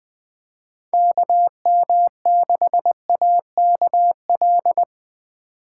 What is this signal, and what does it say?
Morse code, heard as a keyed tone, about 23 wpm, 700 Hz. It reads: KM6AKL
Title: Speed Racing - USA 2x3 Call Signs 20wpm